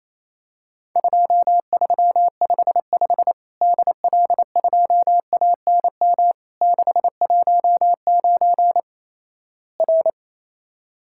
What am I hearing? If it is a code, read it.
Morse code, heard as a keyed tone, about 28 wpm, 700 Hz. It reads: 2355 DL2ANM 619 R
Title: Worked All Europe DX Contest - QTC Exchange - 28wpm